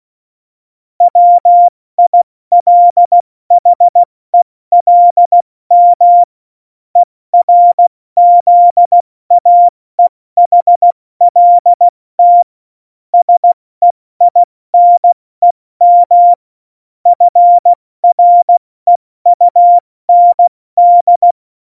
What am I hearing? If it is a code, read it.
Morse code, heard as a keyed tone, about 16 wpm, 700 Hz. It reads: WILHELM ERZAEHLT SEINEM FREUND
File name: Morsekurs / Witze / Witz 13. August 2019, 70 F80